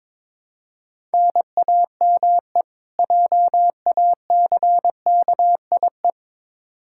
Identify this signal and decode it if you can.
Morse code, heard as a keyed tone, about 22 wpm, 700 Hz. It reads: NAME JACKIE
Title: Practice QSO - Exchange 1 - Beginning - 22wpm